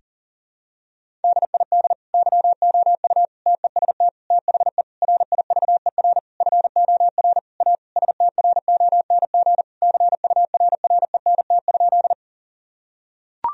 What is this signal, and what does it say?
Morse code, heard as a keyed tone, about 37 wpm, 700 Hz. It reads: DID YOU TEST THE RIVER FOR A STRONG CURRENT?
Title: Sentences from Top 1000 Words - Review Part 4 - 40wpm